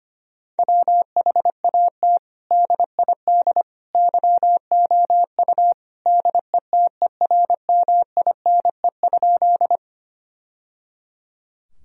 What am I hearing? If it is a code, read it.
Morse code, heard as a keyed tone, about 25 wpm, 700 Hz. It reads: WHAT DID YOU DETERMINE?
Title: Sentences from Top 1000 Words - Review Part 4 - 25wpm